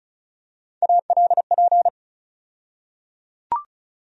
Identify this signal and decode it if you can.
Morse code, heard as a keyed tone, about 34 wpm, 700 Hz. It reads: ALP
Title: USA Call Sign Postfixes - 3 Letters 35wpm